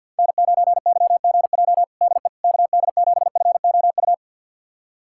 Morse code, heard as a keyed tone, about 51 wpm, 700 Hz. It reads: N9YGJ DE KD7FOU
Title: USA Callsign DE Callsign 50wpm